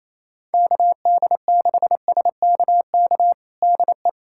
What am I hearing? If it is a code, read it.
Morse code, heard as a keyed tone, about 27 wpm, 700 Hz. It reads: KD6SKK DE
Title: USA Callsign DE Callsign 28wpm